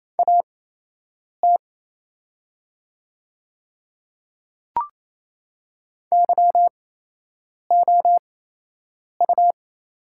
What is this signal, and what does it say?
Morse code, heard as a keyed tone, about 28 wpm, spaced as for 8 wpm, 700 Hz. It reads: AT YOU
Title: 8x Character Spacing - Top 100 Words - 28wpm